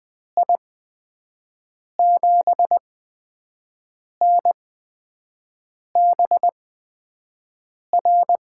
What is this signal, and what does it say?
Morse code, heard as a keyed tone, about 20 wpm, spaced as for 5 wpm, 700 Hz. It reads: I7NBR
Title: USA Call Signs - 8x Character Spacing - 20wpm